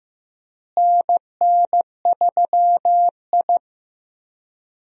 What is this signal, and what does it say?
Morse code, heard as a keyed tone, about 15 wpm, 700 Hz. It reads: NN3I
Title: USA Call Signs - 4 Characters - Repeated 3x - 15wpm